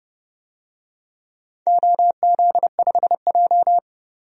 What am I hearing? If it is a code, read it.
Morse code, heard as a keyed tone, about 30 wpm, 700 Hz. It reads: OZ5J
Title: DXCC-100 Call Signs - 4 Characters - 30wpm